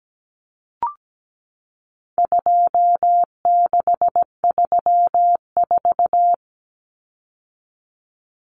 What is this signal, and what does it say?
Morse code, heard as a keyed tone, about 17 wpm, 700 Hz. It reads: 2634
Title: Numbers - 4-Digit 17wpm